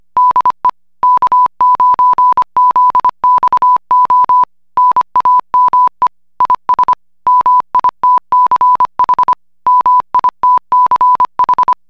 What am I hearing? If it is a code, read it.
Morse code, heard as a keyed tone, about 25 wpm, 1000 Hz. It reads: DE K9ZXO NAME IS MITCH MITCH